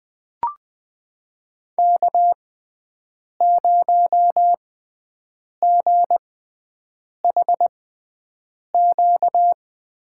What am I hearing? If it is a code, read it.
Morse code, heard as a keyed tone, about 20 wpm, spaced as for 7 wpm, 700 Hz. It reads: K0GHQ
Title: USA Call Signs - 5 Characters - 6x Character Spacing - 20wpm